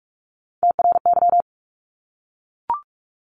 Morse code, heard as a keyed tone, about 44 wpm, 700 Hz. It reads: TRY
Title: ING Suffix - Top 100 Words with Base Words 45wpm